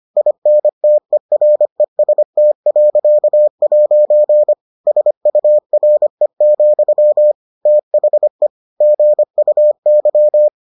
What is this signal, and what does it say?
Morse code, heard as a keyed tone, about 25 wpm, 600 Hz. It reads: INTEREST.' SURE, THE GUY